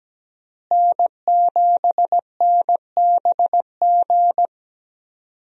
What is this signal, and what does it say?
Morse code, heard as a keyed tone, about 17 wpm, 700 Hz. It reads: N7NBG
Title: USA Call Signs - 5 Characters - Repeated 2x - 17wpm